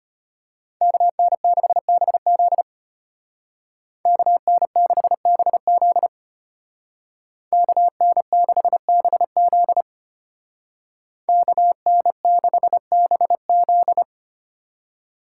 Morse code, heard as a keyed tone, about 30 wpm, 700 Hz. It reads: KN6BZ KN6BZ KN6BZ KN6BZ
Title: Speed Racing - USA Call Signs 25wpm